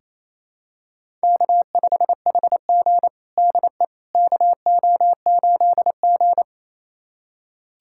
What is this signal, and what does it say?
Morse code, heard as a keyed tone, about 28 wpm, 700 Hz. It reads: K5HG DE KO8G
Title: USA Callsign DE Callsign - 4 Characters - 28wpm